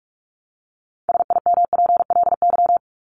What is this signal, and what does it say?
Morse code, heard as a keyed tone, about 44 wpm, 700 Hz. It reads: SIMPLY
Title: Top 501-750 Words 45wpm